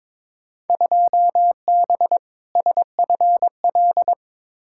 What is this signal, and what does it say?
Morse code, heard as a keyed tone, about 22 wpm, 700 Hz. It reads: 2B SFL
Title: Field Day Exchanges Encore - 22wpm